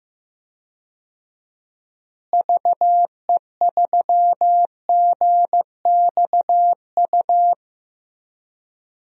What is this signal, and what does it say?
Morse code, heard as a keyed tone, about 15 wpm, 700 Hz. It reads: VE3GXU
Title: DXCC-100 Call Signs - 6 Characters - Repeated 2x - 15wpm